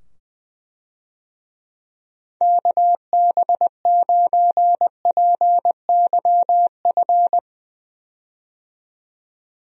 Morse code, heard as a keyed tone, about 21 wpm, 700 Hz. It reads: KB9PYF
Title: USA Call Signs - 6 Characters - 20wpm